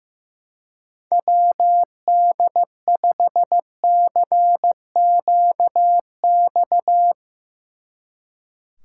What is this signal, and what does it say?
Morse code, heard as a keyed tone, about 15 wpm, 700 Hz. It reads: WD5CQX
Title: Speed Racing - USA 2x3 Call Signs 15wpm